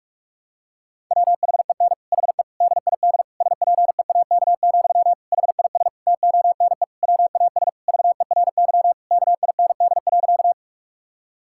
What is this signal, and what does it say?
Morse code, heard as a keyed tone, about 41 wpm, 700 Hz. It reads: WHEN HE DID SPEAK, HIS TONE WAS VERY KIND.
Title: Sentences from Top 1600 Words 45wpm